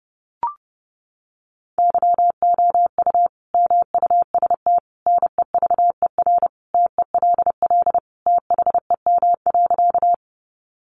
Morse code, heard as a keyed tone, about 30 wpm, 700 Hz. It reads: YOU MUST NEVER TELL THEM.
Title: Sentences from Top 900 Words 30wpm